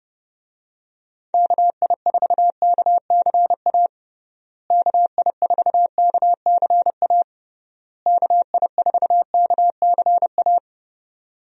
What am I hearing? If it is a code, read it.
Morse code, heard as a keyed tone, about 29 wpm, 700 Hz. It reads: KI4KCA KI4KCA KI4KCA
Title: USA Call Signs - 6 Characters - Repeated 3x - 30wpm